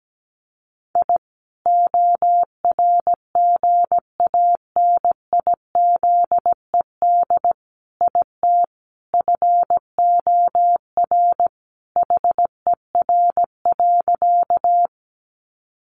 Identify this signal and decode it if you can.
Morse code, heard as a keyed tone, about 17 wpm, 700 Hz. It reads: I ORGANIZED IT FOR HER.
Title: Sentences from Top 1300 Words 17wpm